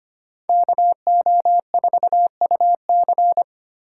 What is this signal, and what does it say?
Morse code, heard as a keyed tone, about 33 wpm, 700 Hz. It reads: KO4UC